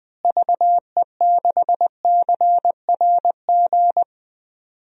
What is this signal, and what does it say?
Morse code, heard as a keyed tone, about 20 wpm, 700 Hz. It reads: VE6CRG
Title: DXCC-100 Call Signs - 6 Characters - Repeated 3x - 20wpm